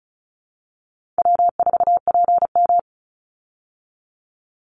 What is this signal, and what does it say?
Morse code, heard as a keyed tone, about 34 wpm, 700 Hz. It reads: W4PM
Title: Call Signs Repeated 3x - 35wpm